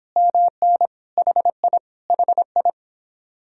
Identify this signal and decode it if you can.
Morse code, heard as a keyed tone, about 26 wpm, 700 Hz. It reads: MN HI HI